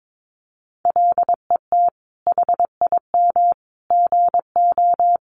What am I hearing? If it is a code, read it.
Morse code, heard as a keyed tone, about 22 wpm, 700 Hz. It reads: LET HIM GO
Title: Sentences from Top 500 Words 22wpm